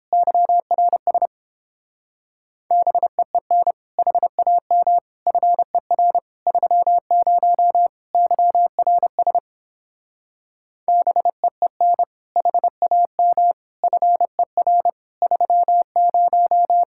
Morse code, heard as a keyed tone, about 29 wpm, 700 Hz. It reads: YRS BEEN HAM FER 30 YRS BEEN HAM FER 30